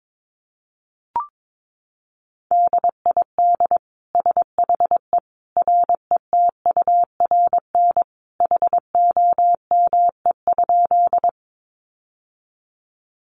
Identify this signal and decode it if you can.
Morse code, heard as a keyed tone, about 22 wpm, 700 Hz. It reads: DID SHE RETURN HOME?
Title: Sentences from Top 800 Words 22wpm